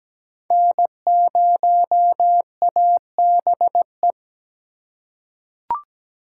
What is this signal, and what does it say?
Morse code, heard as a keyed tone, about 17 wpm, 700 Hz. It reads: N0ABE
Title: USA Call Signs - 4x Character Spacing - 17wpm